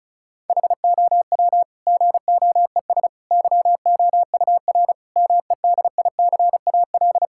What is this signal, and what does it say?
Morse code, heard as a keyed tone, about 36 wpm, 700 Hz. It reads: HOW GOES YOUR MEDICAL
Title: Sentences from Top 1000 Words - Review Part 5 - 35wpm